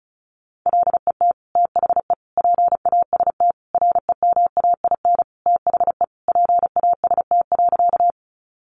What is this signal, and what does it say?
Morse code, heard as a keyed tone, about 34 wpm, 700 Hz. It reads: LET THE PAST REMAIN THE PAST.
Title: Sentences from Top 1200 Words 35wpm